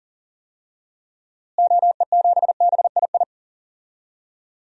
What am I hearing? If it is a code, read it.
Morse code, heard as a keyed tone, about 40 wpm, 700 Hz. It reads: OE7BII
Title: DXCC-100 Call Signs - 6 Characters - Repeated 3x - 40wpm